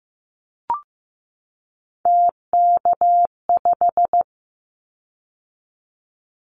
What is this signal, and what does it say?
Morse code, heard as a keyed tone, about 15 wpm, 700 Hz. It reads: TK5
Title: Callsign Prefixes for Top 100 DXCC - 15wpm